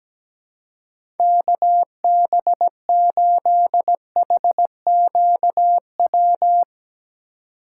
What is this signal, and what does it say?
Morse code, heard as a keyed tone, about 17 wpm, 700 Hz. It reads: KB8HQW
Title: USA Call Signs - 6 Characters - 4x Character Spacing - 17wpm